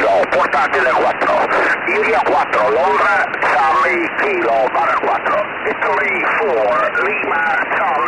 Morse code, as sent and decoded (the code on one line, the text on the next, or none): none